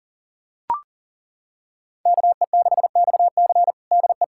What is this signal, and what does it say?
Morse code, heard as a keyed tone, about 40 wpm, 700 Hz. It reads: KE6XC DE
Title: USA Callsign DE Callsign - 5 Characters - 40wpm